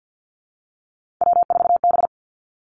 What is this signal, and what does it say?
Morse code, heard as a keyed tone, about 50 wpm, 700 Hz. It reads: W4B
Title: Three Challenging Letter-Number Combinations 50wpm